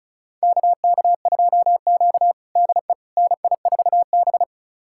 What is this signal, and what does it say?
Morse code, heard as a keyed tone, about 36 wpm, 700 Hz. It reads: KK2Q DE NI4B
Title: USA Callsign DE Callsign - 4 Characters - 35wpm